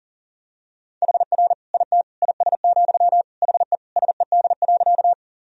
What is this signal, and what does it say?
Morse code, heard as a keyed tone, about 40 wpm, 700 Hz. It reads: HR IT IS, HE SED.
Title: Sentences from Top 200 Words and Abbreviations 40wpm